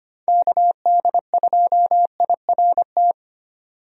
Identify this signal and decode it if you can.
Morse code, heard as a keyed tone, about 25 wpm, 700 Hz. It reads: KD2IRT
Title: USA Call Signs - 6 Characters - 25wpm